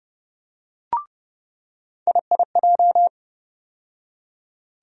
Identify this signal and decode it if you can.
Morse code, heard as a keyed tone, about 31 wpm, 700 Hz. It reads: IIJ
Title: USA Call Sign Postfixes - 3 Letters 30wpm